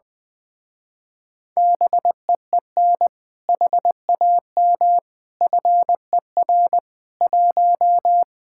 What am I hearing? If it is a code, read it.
Morse code, heard as a keyed tone, about 21 wpm, 700 Hz. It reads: BEEN HAM FER 1
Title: Speed-Racing Common QSO Phrases - 20wpm